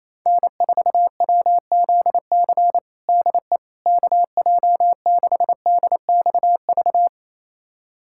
Morse code, heard as a keyed tone, about 28 wpm, 700 Hz. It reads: N4WZC DE KJ6DXV
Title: USA Callsign DE Callsign 28wpm